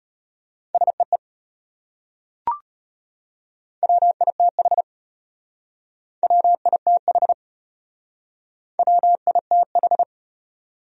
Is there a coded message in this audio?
Yes: Morse code from 0.7 to 10.0 s, heard as a keyed tone, about 32 wpm, 700 Hz. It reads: SEE WITH WITH WITH